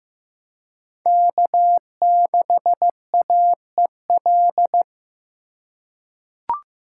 Morse code, heard as a keyed tone, about 15 wpm, 700 Hz. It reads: K6AEL
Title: USA Call Signs - 5 Characters - 3x Character Spacing - 15wpm